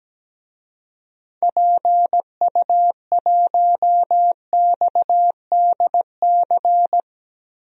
Morse code, heard as a keyed tone, about 17 wpm, 700 Hz. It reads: PU1XDC
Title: DXCC-100 Call Signs - 6 Characters - Repeated 2x - 17wpm